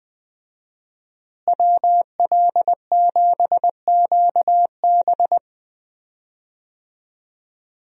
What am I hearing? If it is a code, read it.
Morse code, heard as a keyed tone, about 20 wpm, 700 Hz. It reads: WL7QB
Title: USA Call Signs - 5 Characters - Repeated 2x - 20wpm